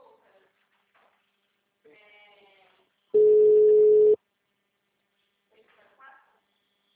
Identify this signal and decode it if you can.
Ringback tone (Europe, 425 Hz), with speech.